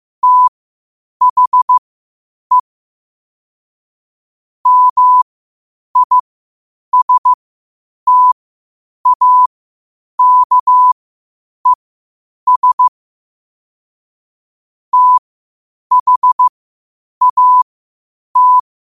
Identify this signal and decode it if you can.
Morse code, heard as a keyed tone, about 15 wpm, spaced as for 8 wpm, 1000 Hz. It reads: THE MISTAKES THAT
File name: qotdfiles / 8wpm